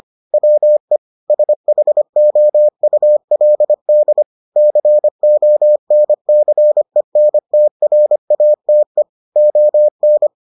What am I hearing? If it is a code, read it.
Morse code, heard as a keyed tone, about 25 wpm, 600 Hz. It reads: WE SHOULD CONCENTRATE ON